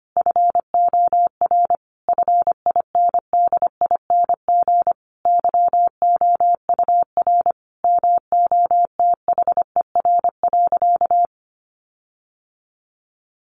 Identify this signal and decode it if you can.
Morse code, heard as a keyed tone, about 25 wpm, 700 Hz. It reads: FOR FINDING YOUR MOTHER.
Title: Sentences from Top 1300 Words 25wpm